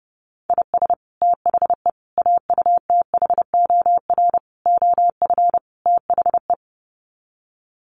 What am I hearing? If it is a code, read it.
Morse code, heard as a keyed tone, about 30 wpm, 700 Hz. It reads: IS THE AUTHOR OF THE
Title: Sets of 5 Words Encore 30wpm